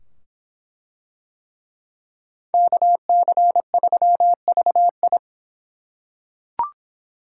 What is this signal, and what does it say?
Morse code, heard as a keyed tone, about 25 wpm, 700 Hz. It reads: KC3VI